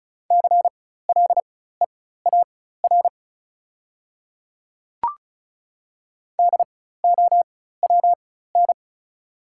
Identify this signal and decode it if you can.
Morse code, heard as a keyed tone, about 35 wpm, spaced as for 16 wpm, 690 Hz. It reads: CLEAR DOWN
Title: 4x Character Spacing - Top 500 Words - 35wpm